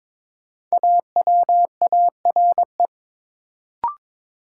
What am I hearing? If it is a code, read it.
Morse code, heard as a keyed tone, about 21 wpm, 700 Hz. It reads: AWARE
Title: Sentences from Top 1200 Words - New Word Drill - 22wpm